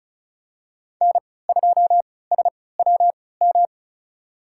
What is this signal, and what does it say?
Morse code, heard as a keyed tone, about 34 wpm, spaced as for 20 wpm, 700 Hz. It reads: N2SWM
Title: USA Call Signs - 5 Characters - 3x Character Spacing - 35wpm